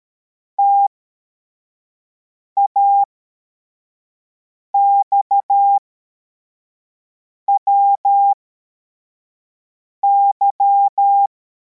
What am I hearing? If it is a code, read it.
Morse code, heard as a keyed tone, about 13 wpm, spaced as for 4 wpm, 800 Hz. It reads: TAXWY
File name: formation / cw / Text63